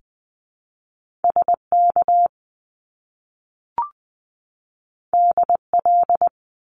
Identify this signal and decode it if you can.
Morse code, heard as a keyed tone, about 20 wpm, 700 Hz. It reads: SK DL
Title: Top Two Letter Combinations in Words 20wpm